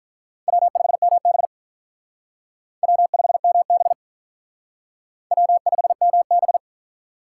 Morse code, heard as a keyed tone, about 43 wpm, 700 Hz. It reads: W5MB W5MB W5MB
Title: Speed Racing - USA Call Signs 35wpm